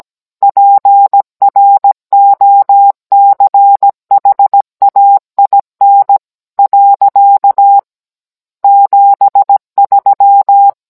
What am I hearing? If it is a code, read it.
Morse code, heard as a keyed tone, about 17 wpm, 800 Hz. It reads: PROCHAIN . 73